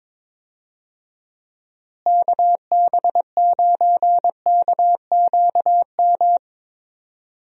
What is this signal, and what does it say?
Morse code, heard as a keyed tone, about 22 wpm, 700 Hz. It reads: KB9KQM
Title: USA Call Signs - 6 Characters - Repeated 2x - 22wpm